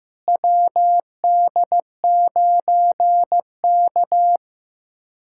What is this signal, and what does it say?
Morse code, heard as a keyed tone, about 15 wpm, 700 Hz. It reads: WD9K
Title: USA Callsign DE Callsign - 4 Characters - 15wpm